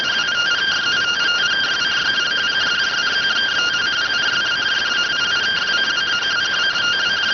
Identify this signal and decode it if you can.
Two-tone teleprinter signal, about 170 Hz shift.